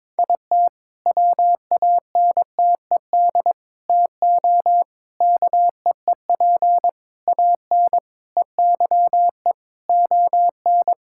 Morse code, heard as a keyed tone, about 21 wpm, 690 Hz. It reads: IT WANTED TO KEEP AN EYE ON
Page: morse-code-ninja-practice-22wpm